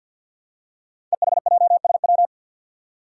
Morse code, heard as a keyed tone, about 49 wpm, 700 Hz. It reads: EHJSW
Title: Code Groups - 5 Letters - 50wpm